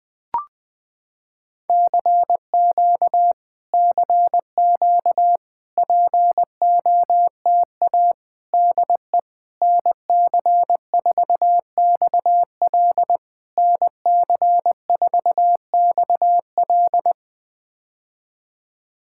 Morse code, heard as a keyed tone, about 20 wpm, 700 Hz. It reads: CQ CQ POTA DE NC4XL NC4XL
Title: POTA QSO Practice - 20wpm